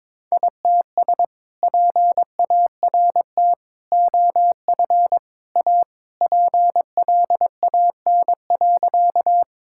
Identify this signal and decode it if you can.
Morse code, heard as a keyed tone, about 21 wpm, 700 Hz. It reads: ITS PART OF A PLAN.
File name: practice / 22wpm / Sentences with 2-Word Phrases - 22wpm